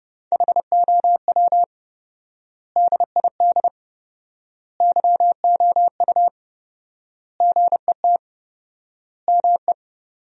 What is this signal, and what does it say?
Morse code, heard as a keyed tone, about 30 wpm, 700 Hz. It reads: HOW DID YOU GET ME
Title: 4x Word-Spacing - Sentences from Top 200 Words - 30wpm